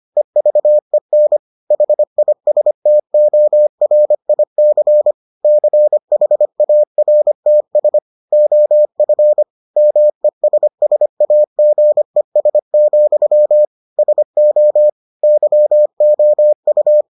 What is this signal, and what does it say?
Morse code, heard as a keyed tone, about 25 wpm, 600 Hz. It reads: EVEN HISTORIC CHARTS OF MESSAGES, SO YOU